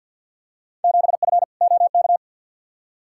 Morse code, heard as a keyed tone, about 47 wpm, 700 Hz. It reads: 7F OK
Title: Field Day Exchanges Encore - 50wpm